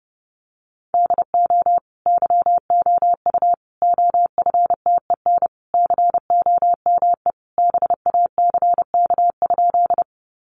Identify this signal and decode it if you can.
Morse code, heard as a keyed tone, about 30 wpm, 700 Hz. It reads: DO YOU OFTEN COME BACK?
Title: Sentences from Top 300 Words 30wpm